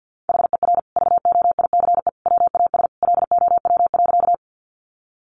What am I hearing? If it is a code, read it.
Morse code, heard as a keyed tone, about 50 wpm, 700 Hz. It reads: HER VOICE WAS LOW.
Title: Sentences from Top 500 Words 50wpm